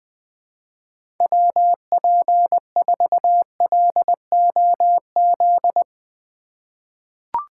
Morse code, heard as a keyed tone, about 20 wpm, 700 Hz. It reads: WP4LOZ
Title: USA Call Signs - 6 Characters - 4x Character Spacing - 20wpm